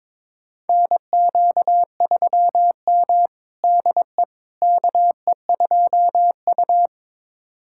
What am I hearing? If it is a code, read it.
Morse code, heard as a keyed tone, about 22 wpm, 700 Hz. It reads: NQ3M DE KE2U
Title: USA Callsign DE Callsign - 4 Characters - 22wpm